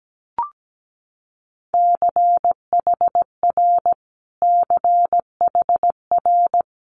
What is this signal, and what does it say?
Morse code, heard as a keyed tone, about 17 wpm, 700 Hz. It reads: CHR CHR
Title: Multi-letter Phonemes Repeated 3x 17wpm